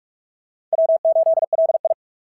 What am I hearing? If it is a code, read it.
Morse code, heard as a keyed tone, about 44 wpm, 660 Hz. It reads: W8LI
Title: SOTA QSO Practice 45wpm